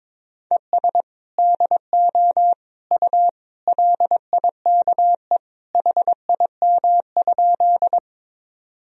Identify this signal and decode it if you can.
Morse code, heard as a keyed tone, about 22 wpm, 700 Hz. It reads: ES DO U LIKE HIM?
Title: Sentences from Top 100 Words and Abbreviations 22wpm